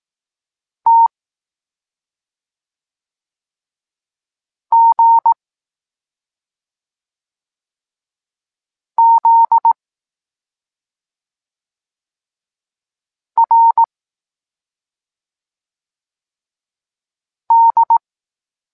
Morse code, heard as a keyed tone, about 18 wpm, 920 Hz. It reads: T G Z R D
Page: Tekst 23.3 – Ciąg dalszy doskonalenia odbioru poznanych liter